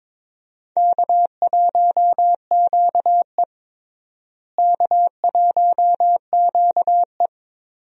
Morse code, heard as a keyed tone, about 22 wpm, 700 Hz. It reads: K1QE K1QE
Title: USA Call Signs - 4 Characters - Repeated 2x - 22wpm